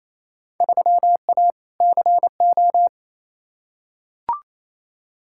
Morse code, heard as a keyed tone, about 28 wpm, 700 Hz. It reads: 3A CO
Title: Field Day Exchanges Encore - 28wpm